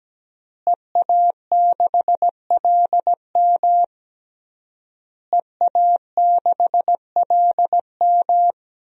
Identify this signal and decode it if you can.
Morse code, heard as a keyed tone, about 17 wpm, 700 Hz. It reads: EA6LM EA6LM
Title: DXCC-100 Call Signs - 5 Characters - Repeated 3x - 17wpm